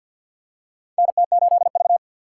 Morse code, heard as a keyed tone, about 48 wpm, 700 Hz. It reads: NT8V
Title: USA Call Signs - 3x Character Spacing - 50wpm